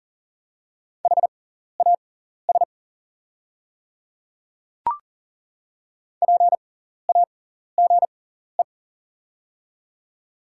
Morse code, heard as a keyed tone, about 37 wpm, spaced as for 14 wpm, 690 Hz. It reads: HAS PAGE
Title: 6x Character Spacing - Top 300 Words - 40wpm